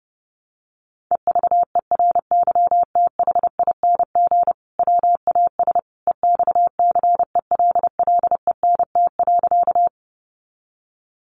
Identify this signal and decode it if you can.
Morse code, heard as a keyed tone, about 30 wpm, 700 Hz. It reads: EVERYTHING WAS EXCELLENT.
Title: Sentences from Top 1000 Words 30wpm